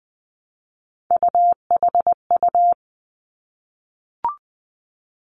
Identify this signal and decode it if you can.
Morse code, heard as a keyed tone, about 20 wpm, 700 Hz. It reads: UHU
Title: Three Challenging Letter-Number Combinations 20wpm